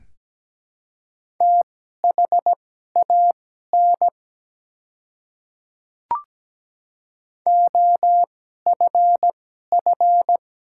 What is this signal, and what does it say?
Morse code, heard as a keyed tone, about 17 wpm, spaced as for 12 wpm, 700 Hz. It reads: THAN OFF
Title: 2x Character Spacing - Top 200 Words - 17wpm